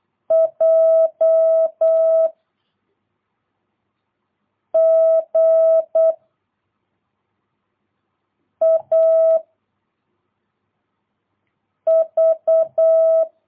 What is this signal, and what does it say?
Morse code, heard as a keyed tone, about 8 wpm, 660 Hz. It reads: J G A V
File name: jh8cbh / 2022cw